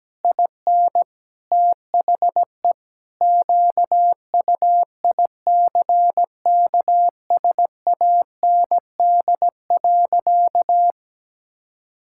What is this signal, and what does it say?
Morse code, heard as a keyed tone, about 17 wpm, 700 Hz. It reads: IN THE QUICKSAND.
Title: Pangram Challenge v2 - 17wpm